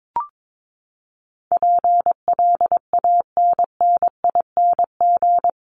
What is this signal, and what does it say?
Morse code, heard as a keyed tone, about 22 wpm, 700 Hz. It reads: PLANNING
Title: ING Suffix - Top 100 Words with Base Words 22wpm